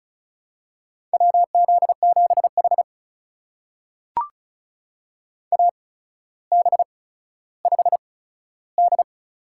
Morse code, heard as a keyed tone, about 36 wpm, spaced as for 15 wpm, 700 Hz. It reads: WZ7H AB5D
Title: USA Call Signs - 4 Characters - 8x Character Spacing - 35wpm